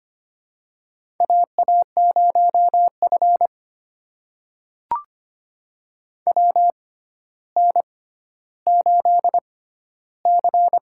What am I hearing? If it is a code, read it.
Morse code, heard as a keyed tone, about 25 wpm, spaced as for 13 wpm, 700 Hz. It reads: AA0F WN8C
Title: USA Call Signs - 4 Characters - 6x Character Spacing - 25wpm